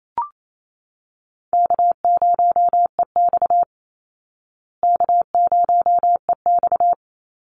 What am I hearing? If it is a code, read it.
Morse code, heard as a keyed tone, about 28 wpm, 700 Hz. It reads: K0EX K0EX
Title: Easy Call Signs Repeated 2x - 28wpm